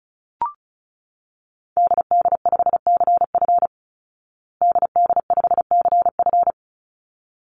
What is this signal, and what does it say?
Morse code, heard as a keyed tone, about 34 wpm, 700 Hz. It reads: DD5CF DD5CF
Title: Call Signs Repeated 3x - 35wpm